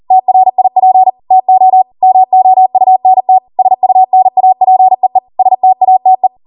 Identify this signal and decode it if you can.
Morse code, heard as a keyed tone, about 40 wpm, 750 Hz. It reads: TRIP TO MOUNT SUNAPEE STATE